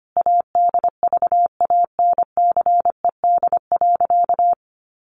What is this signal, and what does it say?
Morse code, heard as a keyed tone, about 25 wpm, 700 Hz. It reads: ADVANCED.
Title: Sentences from Top 1200 Words 25wpm